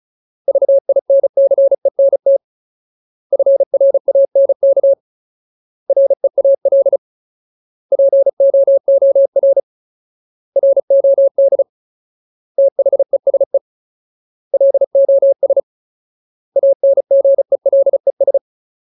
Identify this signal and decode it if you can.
Morse code, heard as a keyed tone, about 34 wpm, 550 Hz. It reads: VINCENT FRANK REAL POOR ROD THESE LOS ANGELES
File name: Ultimate Word List / 35WPM Ultimate